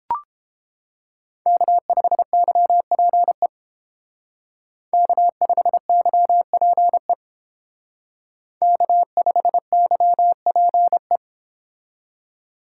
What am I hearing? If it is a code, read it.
Morse code, heard as a keyed tone, about 30 wpm, 700 Hz. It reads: K5YPE K5YPE K5YPE